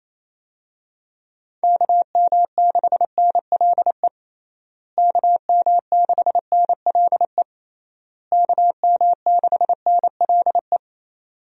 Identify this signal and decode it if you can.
Morse code, heard as a keyed tone, about 28 wpm, 700 Hz. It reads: KM6NLE KM6NLE KM6NLE
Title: USA Call Signs - 6 Characters - Repeated 3x - 28wpm